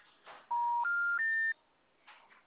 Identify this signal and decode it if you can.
Special information tone.
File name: out- -2009- - - .136